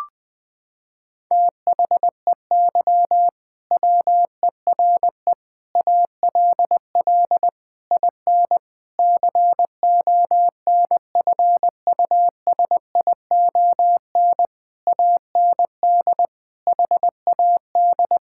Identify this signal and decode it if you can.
Morse code, heard as a keyed tone, about 20 wpm, 700 Hz. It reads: THEY WERE ALL IN CONFUSION AND HAD
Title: Sentences from Top 1600 Words 20wpm